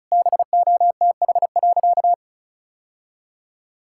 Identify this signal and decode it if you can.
Morse code, heard as a keyed tone, about 36 wpm, 700 Hz. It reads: BOTH.
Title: Sentences from Top 1000 Words - Review Part 6 - 35wpm